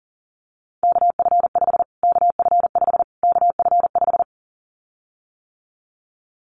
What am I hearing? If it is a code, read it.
Morse code, heard as a keyed tone, about 40 wpm, 700 Hz. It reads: KF5 KF5 KF5
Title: Callsign Prefixes for Top 25 DXCC - Repeated 3x - 40wpm